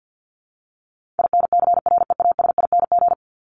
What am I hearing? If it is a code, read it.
Morse code, heard as a keyed tone, about 49 wpm, 700 Hz. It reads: INCREASING
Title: ING Suffix - Top 100 Words with Base Words 50wpm